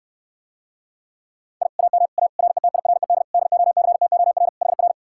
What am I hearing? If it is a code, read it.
Morse code, heard as a keyed tone, about 69 wpm, 700 Hz. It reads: I AM A RETIRED DOCTOR HR